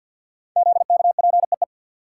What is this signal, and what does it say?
Morse code, heard as a keyed tone, about 52 wpm, 700 Hz. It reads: ZKPEE
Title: Code Groups - 5 Letters - 50wpm